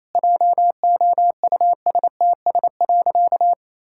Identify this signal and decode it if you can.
Morse code, heard as a keyed tone, about 28 wpm, 700 Hz. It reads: JOUSTS.
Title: Pangram Challenge v2 - 28wpm